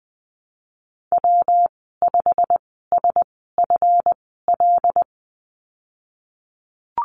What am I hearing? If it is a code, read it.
Morse code, heard as a keyed tone, about 20 wpm, spaced as for 14 wpm, 700 Hz. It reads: W5SFL
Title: 161 Lesson 26g - Letter g - Call Signs 20wpm